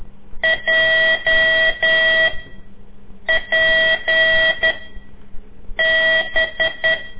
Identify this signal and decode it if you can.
Morse code, heard as a keyed tone, about 10 wpm, 1950 Hz, with noise. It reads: J P B
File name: CORSI CW